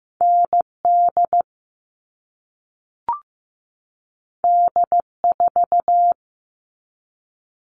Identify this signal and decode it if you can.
Morse code, heard as a keyed tone, about 15 wpm, 700 Hz. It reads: ND D4
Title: All Cut Numbers - 2-Digit 15wpm